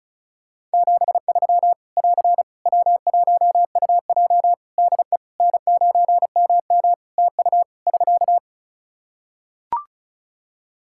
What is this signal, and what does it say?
Morse code, heard as a keyed tone, about 36 wpm, 700 Hz. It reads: 73 <AR> W1UJ DE N9MM TU <SK>
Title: Practice QSO - Exchange 4 - Ending - 35wpm